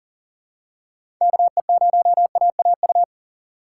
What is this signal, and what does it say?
Morse code, heard as a keyed tone, about 39 wpm, 700 Hz. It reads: KE0AAU
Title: USA Call Signs - 2x Character Spacing - 40wpm